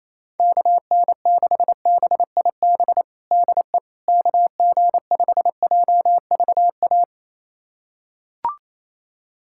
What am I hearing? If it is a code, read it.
Morse code, heard as a keyed tone, about 27 wpm, 700 Hz. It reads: KN6BIB DE KG5JVA
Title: USA Callsign DE Callsign - 6 Characters - 28wpm